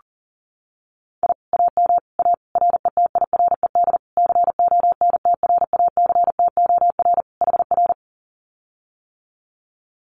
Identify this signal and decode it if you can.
Morse code, heard as a keyed tone, about 38 wpm, 700 Hz. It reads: I AM A RETIRED CONTRACTOR HR
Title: Common QSO Phrases Encore 40wpm